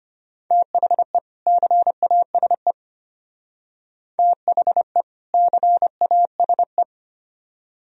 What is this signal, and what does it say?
Morse code, heard as a keyed tone, about 32 wpm, 700 Hz. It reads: THE CASE THE CASE